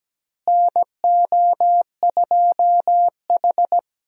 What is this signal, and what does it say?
Morse code, heard as a keyed tone, about 17 wpm, 700 Hz. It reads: NO2H
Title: USA Call Signs - 4 Characters - Repeated 2x - 17wpm